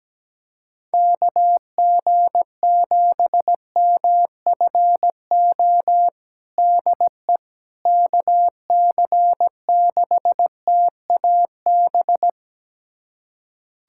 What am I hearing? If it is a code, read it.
Morse code, heard as a keyed tone, about 17 wpm, 700 Hz. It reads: KG7MFO DE KC6TAB
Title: USA Callsign DE Callsign 17wpm